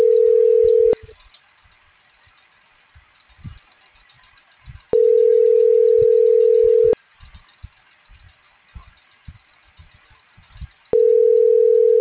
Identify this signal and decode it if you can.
Ringback tone (North America), 440 + 480 Hz.